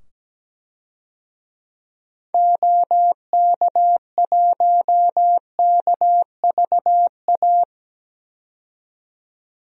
Morse code, heard as a keyed tone, about 17 wpm, 700 Hz. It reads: OK1KVA
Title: DXCC-100 Call Signs - 6 Characters - 17wpm